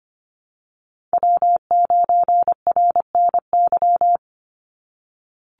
Morse code, heard as a keyed tone, about 25 wpm, 700 Hz. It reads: W9RNY
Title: Call Signs - 25wpm